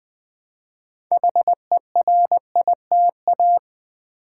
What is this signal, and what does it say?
Morse code, heard as a keyed tone, about 21 wpm, 700 Hz. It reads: HERITA
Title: Build A Word - 3 Letters At A Time - 8-Letter Words - 20wpm